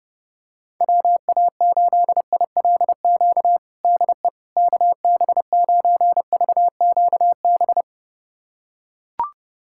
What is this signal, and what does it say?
Morse code, heard as a keyed tone, about 32 wpm, 700 Hz. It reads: WA8ILQ DE KB9VQB
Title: USA Callsign DE Callsign 30wpm